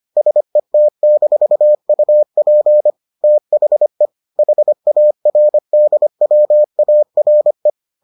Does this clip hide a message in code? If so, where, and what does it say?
Yes: Morse code from 0.2 to 7.7 s, heard as a keyed tone, about 25 wpm, 600 Hz. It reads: SET-UP THE HARDWARE